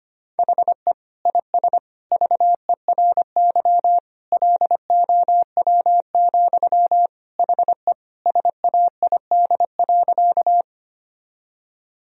Morse code, heard as a keyed tone, about 25 wpm, 700 Hz. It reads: HE IS VERY LOW, HE SAID.
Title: Sentences from Top 1000 Words - Review Part 1 - 25wpm